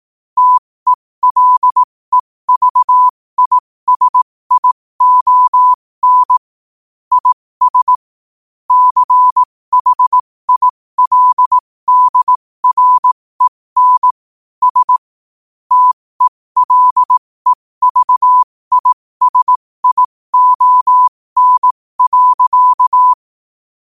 Morse code, heard as a keyed tone, about 18 wpm, spaced as for 15 wpm, 1000 Hz. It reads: TELEVISION IS CHILDREN S TELEVISION.
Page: New quotes every day in morse code at 15 Words per minute.